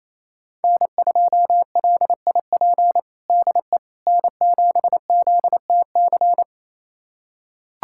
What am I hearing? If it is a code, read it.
Morse code, heard as a keyed tone, about 28 wpm, 700 Hz. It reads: N2LIP DE N7ZTC
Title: USA Callsign DE Callsign - 5 Characters - 28wpm